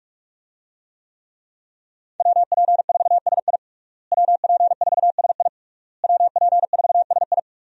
Morse code, heard as a keyed tone, about 46 wpm, 700 Hz. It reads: WP4SI WP4SI WP4SI
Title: USA Call Signs - 5 Characters - Repeated 3x - 45wpm